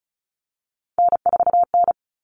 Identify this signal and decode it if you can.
Morse code, heard as a keyed tone, about 34 wpm, 700 Hz. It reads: N4N
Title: All Cut Numbers - 3-Digit 35wpm